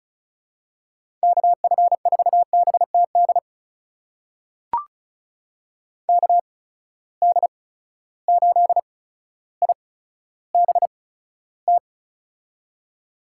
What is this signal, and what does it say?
Morse code, heard as a keyed tone, about 36 wpm, spaced as for 15 wpm, 700 Hz. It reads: KF4BTD KD8IBT
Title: USA Call Signs - 6 Characters - 8x Character Spacing - 35wpm